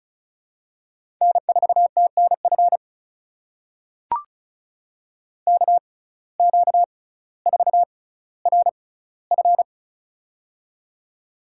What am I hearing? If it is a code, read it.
Morse code, heard as a keyed tone, about 36 wpm, spaced as for 18 wpm, 700 Hz. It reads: N4TNF KQ4RF
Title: USA Call Signs - 5 Characters - 6x Character Spacing - 35wpm